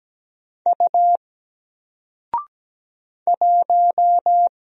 Morse code, heard as a keyed tone, about 17 wpm, 700 Hz. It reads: U 1
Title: Single Letter-Number 17wpm